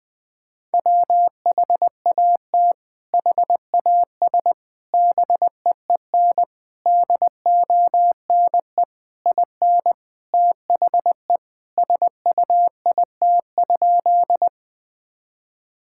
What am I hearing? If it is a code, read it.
Morse code, heard as a keyed tone, about 21 wpm, 700 Hz. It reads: WHAT HAS BEEN DONE IN THE SUIT?
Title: Sentences from Top 1600 Words 20wpm